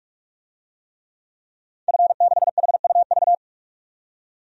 Morse code, heard as a keyed tone, about 46 wpm, 700 Hz. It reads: F6HUV
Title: DXCC-100 Call Signs - 5 Characters - 45wpm